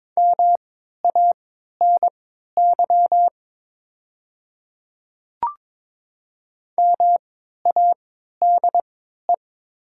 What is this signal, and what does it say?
Morse code, heard as a keyed tone, about 22 wpm, spaced as for 12 wpm, 700 Hz. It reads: MANY MADE
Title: 3x Character Spacing - Top 200 Words - 22wpm